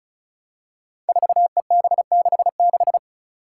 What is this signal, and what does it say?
Morse code, heard as a keyed tone, about 36 wpm, 700 Hz. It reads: 4EB66
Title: Five Challenging Letter-Number Combinations 35wpm